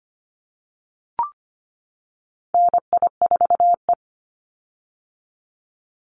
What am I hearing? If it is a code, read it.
Morse code, heard as a keyed tone, about 25 wpm, 700 Hz. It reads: NI4E
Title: Call Signs - 25wpm